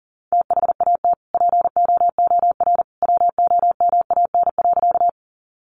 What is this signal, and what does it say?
Morse code, heard as a keyed tone, about 40 wpm, 700 Hz. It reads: THAT POOR WOMAN.
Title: Sentences from Top 1100 Words 40wpm